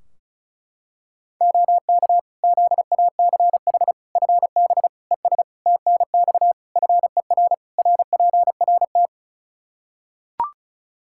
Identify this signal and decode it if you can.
Morse code, heard as a keyed tone, about 36 wpm, 700 Hz. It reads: OK ZACH FB ES TNX FER RPRT
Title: Practice QSO - Exchange 2 - Info - 35wpm